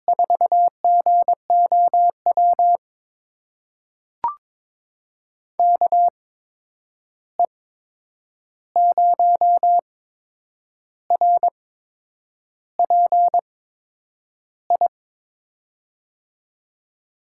Morse code, heard as a keyed tone, about 22 wpm, spaced as for 8 wpm, 700 Hz. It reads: 4GOW KE0RPI
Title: USA Call Signs - 6 Characters - 8x Character Spacing - 22wpm